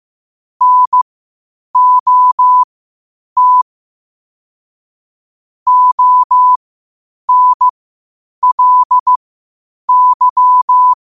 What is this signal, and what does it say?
Morse code, heard as a keyed tone, about 15 wpm, spaced as for 8 wpm, 1000 Hz. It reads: NOT ONLY
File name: qotdfiles / 8wpm